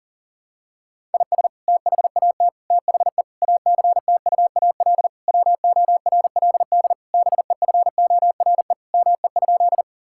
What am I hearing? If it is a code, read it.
Morse code, heard as a keyed tone, about 40 wpm, 700 Hz. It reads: IS THAT THE ACTUAL WORLD BEFORE ME?
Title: Sentences from Top 1000 Words - Review Part 5 - 40wpm